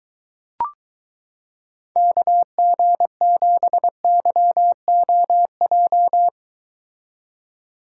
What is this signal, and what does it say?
Morse code, heard as a keyed tone, about 22 wpm, 700 Hz. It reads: KG7YOJ